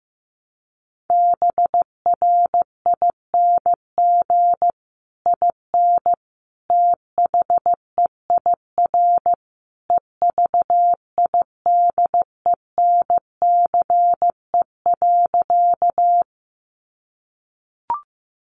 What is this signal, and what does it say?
Morse code, heard as a keyed tone, about 15 wpm, 700 Hz. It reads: BRING IN THEIR EVIDENCE.
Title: Sentences from Top 1300 Words 15wpm